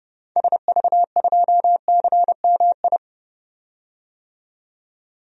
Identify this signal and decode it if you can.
Morse code, heard as a keyed tone, about 31 wpm, 700 Hz. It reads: SV2CMI
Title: DXCC-100 Call Signs - 6 Characters - 30wpm